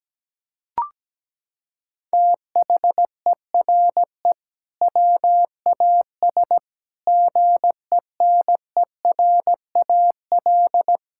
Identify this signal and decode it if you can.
Morse code, heard as a keyed tone, about 17 wpm, 700 Hz. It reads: THERE WAS GENERAL
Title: Sentences from Top 1600 Words 17wpm